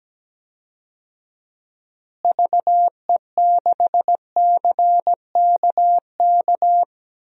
Morse code, heard as a keyed tone, about 17 wpm, 700 Hz. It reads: VE6CKK
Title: DXCC-100 Call Signs - 6 Characters - Repeated 3x - 17wpm